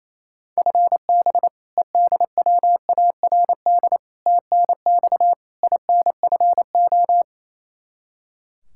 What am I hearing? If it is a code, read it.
Morse code, heard as a keyed tone, about 27 wpm, 700 Hz. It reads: FB EDWARD TNX INFO
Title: Practice QSO - Exchange 3 - More Info - 28wpm